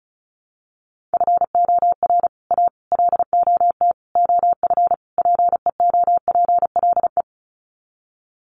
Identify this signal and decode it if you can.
Morse code, heard as a keyed tone, about 34 wpm, 700 Hz. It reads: FOR A LOT OF PEOPLE
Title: Sets of 5 Words Encore 35wpm